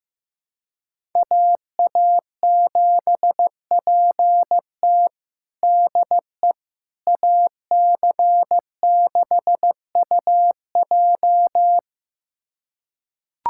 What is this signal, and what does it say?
Morse code, heard as a keyed tone, about 16 wpm, 700 Hz. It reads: AA7PT DE AC6UJ
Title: USA Callsign DE Callsign - 5 Characters - 15wpm